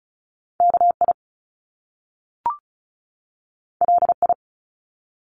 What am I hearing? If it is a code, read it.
Morse code, heard as a keyed tone, about 36 wpm, 700 Hz. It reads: KI LI
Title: Top Two Letter Combinations in Words 35wpm